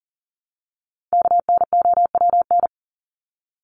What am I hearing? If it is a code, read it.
Morse code, heard as a keyed tone, about 40 wpm, 700 Hz. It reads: KNOWN
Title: Top 501-750 Words 40wpm